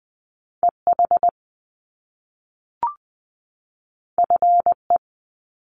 Morse code, heard as a keyed tone, about 20 wpm, 700 Hz. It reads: EH FE
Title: Top Two Letter Combinations in Words 20wpm